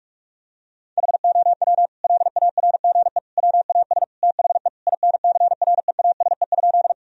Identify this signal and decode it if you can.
Morse code, heard as a keyed tone, about 46 wpm, 700 Hz. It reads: HOW LARGE WAS THE INCREASE?
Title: Sentences from Top 1000 Words - Review Part 4 - 45wpm